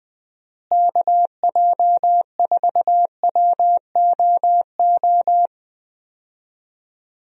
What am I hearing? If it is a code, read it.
Morse code, heard as a keyed tone, about 20 wpm, 700 Hz. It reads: KJ4WOO
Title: USA Call Signs - 6 Characters - Repeated 3x - 20wpm